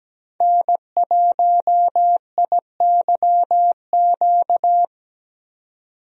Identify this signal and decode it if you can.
Morse code, heard as a keyed tone, about 17 wpm, 700 Hz. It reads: N1IYQ
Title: USA Call Signs - 5 Characters - 17wpm